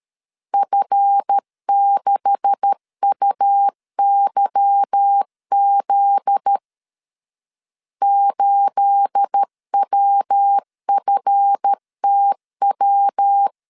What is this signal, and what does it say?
Morse code, heard as a keyed tone, about 13 wpm, 800 Hz. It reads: F6UYZ 8WFTW
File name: CW / DET / 1 9 1500C S A 12wpm 3800